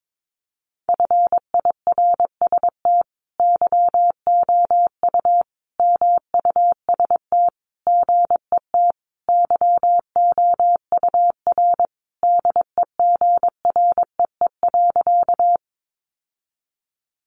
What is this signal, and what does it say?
Morse code, heard as a keyed tone, about 22 wpm, 700 Hz. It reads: FIRST YOU MUST GET YOUR DEGREE.
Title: Sentences from Top 1500 Words 22wpm